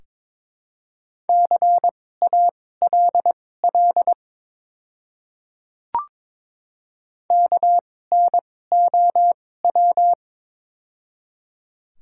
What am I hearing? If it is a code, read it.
Morse code, heard as a keyed tone, about 22 wpm, spaced as for 16 wpm, 690 Hz. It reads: CALL KNOW
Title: 2x Character Spacing - Top 100 Words - 22wpm